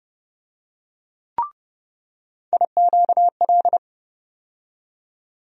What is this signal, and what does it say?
Morse code, heard as a keyed tone, about 30 wpm, 700 Hz. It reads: IQL
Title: USA Call Sign Postfixes - 3 Letters 30wpm